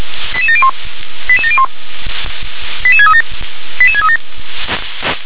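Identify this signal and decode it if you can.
Five-tone selective call (ZVEI), heard as quick runs of five tones, tones of about 70 ms. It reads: 88701 88701 88637 88637
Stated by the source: ZVEI five-tone call received by SDR.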